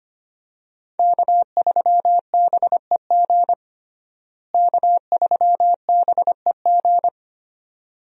Morse code, heard as a keyed tone, about 25 wpm, 700 Hz. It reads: K3BEG K3BEG
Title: USA Call Signs - 5 Characters - Repeated 2x - 25wpm